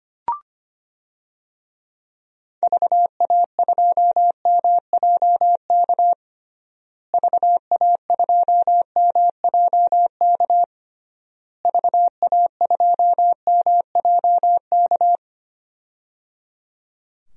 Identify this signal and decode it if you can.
Morse code, heard as a keyed tone, about 25 wpm, 700 Hz. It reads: VA2MJK VA2MJK VA2MJK
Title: DXCC-100 Call Signs - 6 Characters - Repeated 3x - 25wpm